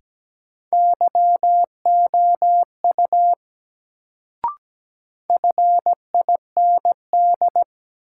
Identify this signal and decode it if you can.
Morse code, heard as a keyed tone, about 17 wpm, 700 Hz. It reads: YOU FIND
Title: Sentences from Top 100 Words - New Word Drill - 17wpm